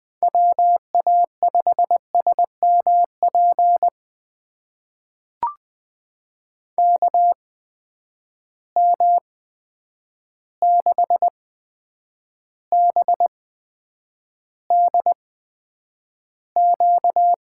Morse code, heard as a keyed tone, about 20 wpm, spaced as for 9 wpm, 700 Hz. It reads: WA5SMP KM6BDQ
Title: USA Call Signs - 6 Characters - 8x Character Spacing - 20wpm